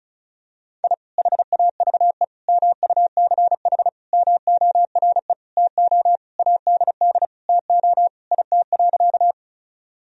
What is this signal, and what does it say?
Morse code, heard as a keyed tone, about 34 wpm, 700 Hz. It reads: I HAVE MUCH MORE TO ADD TO IT.
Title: Sentences from Top 1000 Words - Review Part 1 - 35wpm